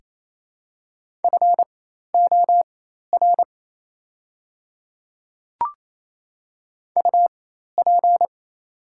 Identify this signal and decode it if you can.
Morse code, heard as a keyed tone, about 28 wpm, spaced as for 13 wpm, 700 Hz. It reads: FOR UP
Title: 4x Character Spacing - Top 100 Words - 28wpm